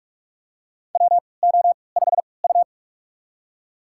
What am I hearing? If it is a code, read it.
Morse code, heard as a keyed tone, about 41 wpm, spaced as for 25 wpm, 690 Hz. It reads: WO5U
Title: USA Call Signs - 4 Characters - 3x Character Spacing - 45wpm